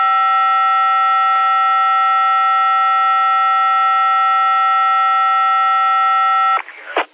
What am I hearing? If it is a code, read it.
SSTV (slow-scan television), Martin 1.